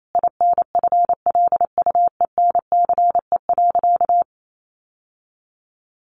Morse code, heard as a keyed tone, about 28 wpm, 700 Hz. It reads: INFLUENCE.
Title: Sentences from Top 1100 Words 28wpm